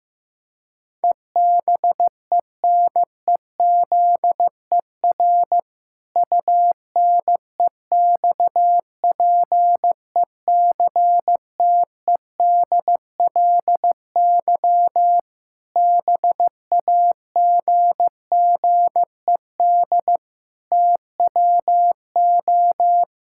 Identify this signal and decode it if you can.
Morse code, heard as a keyed tone, about 16 wpm, 700 Hz. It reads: EBENEZER UNEXPECTEDLY BAGGED TWO
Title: Pangram Challenge v2 - 15wpm